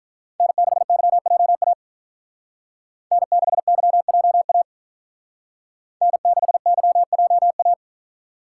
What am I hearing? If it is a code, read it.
Morse code, heard as a keyed tone, about 46 wpm, 700 Hz. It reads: N6YJA N6YJA N6YJA